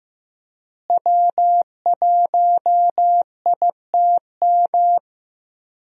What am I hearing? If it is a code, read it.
Morse code, heard as a keyed tone, about 15 wpm, 700 Hz. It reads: W1ITM
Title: USA Call Signs - 5 Characters - Repeated 2x - 15wpm